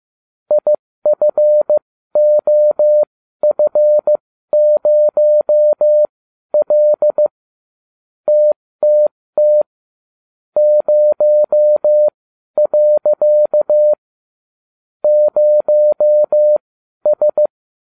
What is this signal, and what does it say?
Morse code, heard as a keyed tone, about 15 wpm, spaced as for 12 wpm, 600 Hz. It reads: IFOF0L TTT 0. 0S
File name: cw / ubacw20 3